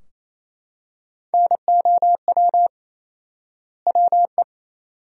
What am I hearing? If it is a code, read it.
Morse code, heard as a keyed tone, about 28 wpm, 700 Hz. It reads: NOW WE
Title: 4x Word-Spacing - Sentences from Top 500 Words - 28wpm